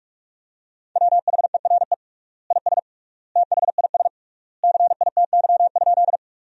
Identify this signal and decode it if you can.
Morse code, heard as a keyed tone, about 41 wpm, 700 Hz. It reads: WHERE IS THIS CITY?
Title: 3x Word-Spacing - Sentences from Top 300 Words - 45wpm